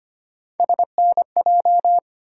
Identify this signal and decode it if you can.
Morse code, heard as a keyed tone, about 25 wpm, 700 Hz. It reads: SNJ
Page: morse-code-ninja-practice-25wpm